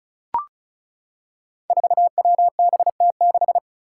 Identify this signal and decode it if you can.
Morse code, heard as a keyed tone, about 36 wpm, 700 Hz. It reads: 4WBT6
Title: Five Challenging Letter-Number Combinations 35wpm